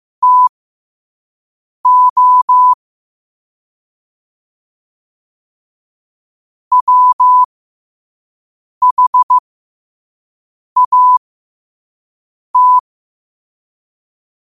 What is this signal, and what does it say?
Morse code, heard as a keyed tone, about 15 wpm, spaced as for 5 wpm, 1000 Hz. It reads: TO WHAT
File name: qotdfiles / 5wpm